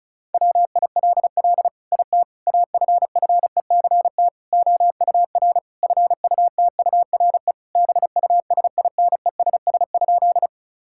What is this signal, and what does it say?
Morse code, heard as a keyed tone, about 36 wpm, 700 Hz. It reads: WILL IT AFFECT OUR FUTURE BUSINESS?
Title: Sentences from Top 1000 Words - Review Part 6 - 35wpm